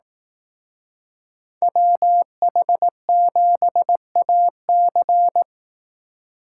Morse code, heard as a keyed tone, about 18 wpm, 700 Hz. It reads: WH7AC
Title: Speed Racing - USA Call Signs 15wpm